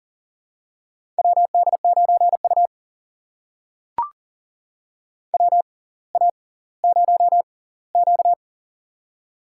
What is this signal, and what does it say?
Morse code, heard as a keyed tone, about 40 wpm, spaced as for 20 wpm, 690 Hz. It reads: WD9U WA0Q
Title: USA Call Signs - 4 Characters - 6x Character Spacing - 40wpm